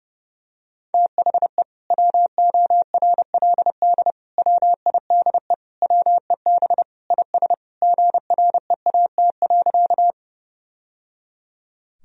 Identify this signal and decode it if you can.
Morse code, heard as a keyed tone, about 32 wpm, 700 Hz. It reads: THE WORLD WIDE WEB IS GREAT.
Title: Sentences from Top 1000 Words - Review Part 4 - 30wpm